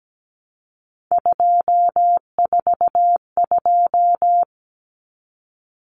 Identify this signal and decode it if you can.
Morse code, heard as a keyed tone, about 17 wpm, 700 Hz. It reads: 242
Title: Numbers - 3-Digit 17wpm